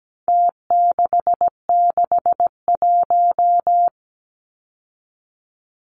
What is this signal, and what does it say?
Morse code, heard as a keyed tone, about 17 wpm, 700 Hz. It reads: T661
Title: Numbers - Contest Style - 4-Digit 17wpm